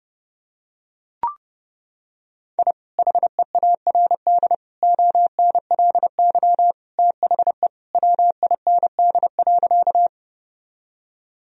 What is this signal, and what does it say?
Morse code, heard as a keyed tone, about 30 wpm, 700 Hz. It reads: I HEARD ONLY THE WIND.
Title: Sentences from Top 1000 Words - Review Part 2 - 30wpm